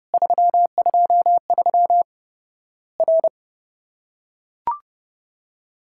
Morse code, heard as a keyed tone, about 30 wpm, 700 Hz. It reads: 323 R
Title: Worked All Europe DX Contest - QTC Exchange - 30wpm